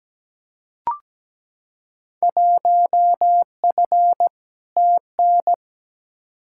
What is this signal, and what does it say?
Morse code, heard as a keyed tone, about 17 wpm, 700 Hz. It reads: 1F TN
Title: Field Day Exchanges - 17wpm